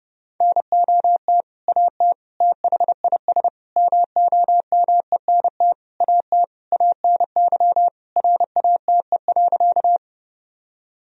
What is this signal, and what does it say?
Morse code, heard as a keyed tone, about 32 wpm, 700 Hz. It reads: NOT AT THIS MOMENT AT ANY RATE.
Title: Sentences from Top 1000 Words - Review Part 4 - 30wpm